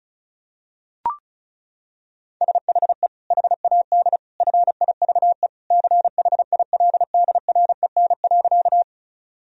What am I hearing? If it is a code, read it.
Morse code, heard as a keyed tone, about 34 wpm, 700 Hz. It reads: SHE HAD FIVE CHILDREN.
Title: Sentences from Top 1000 Words - Review Part 2 - 35wpm